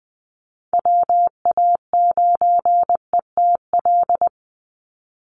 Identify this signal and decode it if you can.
Morse code, heard as a keyed tone, about 20 wpm, 700 Hz. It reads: WA9ETL
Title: USA Call Signs 20wpm